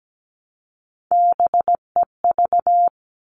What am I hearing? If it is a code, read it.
Morse code, heard as a keyed tone, about 17 wpm, 700 Hz. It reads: BEV
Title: All Cut Numbers - 3-Digit 17wpm